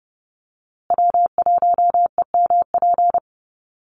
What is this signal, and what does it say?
Morse code, heard as a keyed tone, about 30 wpm, 700 Hz. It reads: W1EMP
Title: USA Call Signs Repeated 3x 30wpm